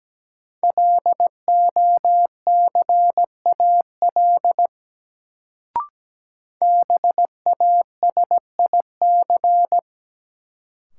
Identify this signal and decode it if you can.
Morse code, heard as a keyed tone, about 17 wpm, 700 Hz. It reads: LOCAL BASIC
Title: Sentences from Top 1000 Words - Review Part 3 - New Word Drill - 17wpm